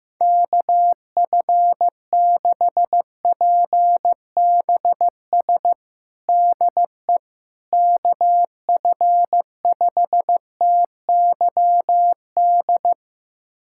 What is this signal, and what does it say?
Morse code, heard as a keyed tone, about 15 wpm, 700 Hz. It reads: KF6PBS DE KF5TYD
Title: USA Callsign DE Callsign - 6 Characters - 15wpm